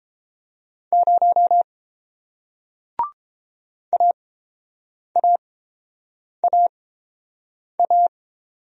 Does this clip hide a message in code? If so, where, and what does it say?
Yes: Morse code from 0.9 to 8.1 s, heard as a keyed tone, about 30 wpm, spaced as for 7 wpm, 690 Hz. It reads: 0 AAAA